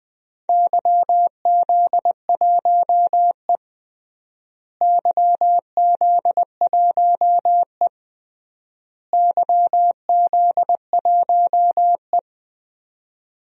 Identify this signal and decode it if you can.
Morse code, heard as a keyed tone, about 20 wpm, 700 Hz. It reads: YZ1E YZ1E YZ1E
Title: DXCC-100 Call Signs - 4 Characters - Repeated 3x - 20wpm